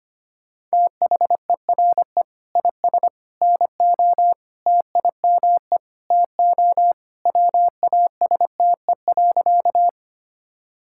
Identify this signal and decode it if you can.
Morse code, heard as a keyed tone, about 25 wpm, 700 Hz. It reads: THERE IS NO TIME TO WASTE.
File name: practice / 25wpm / Sentences with 3-Word Phrases - 25wpm